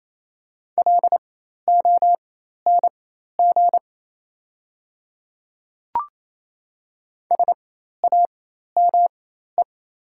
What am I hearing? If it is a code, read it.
Morse code, heard as a keyed tone, about 28 wpm, spaced as for 13 wpm, 700 Hz. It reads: LONG SAME
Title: 4x Character Spacing - Top 200 Words - 28wpm